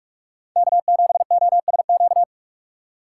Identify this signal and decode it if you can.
Morse code, heard as a keyed tone, about 47 wpm, 700 Hz. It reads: K7OSQ
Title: USA Call Signs - 5 Characters - 6x Character Spacing - 45wpm